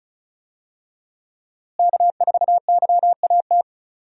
Morse code, heard as a keyed tone, about 36 wpm, 700 Hz. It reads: K4YAT
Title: USA Call Signs - 5 Characters - Repeated 2x - 35wpm